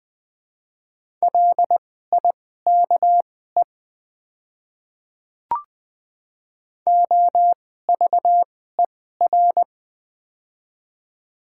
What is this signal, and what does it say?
Morse code, heard as a keyed tone, about 21 wpm, spaced as for 15 wpm, 700 Hz. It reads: LIKE OVER
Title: 2x Character Spacing - Top 100 Words - 20wpm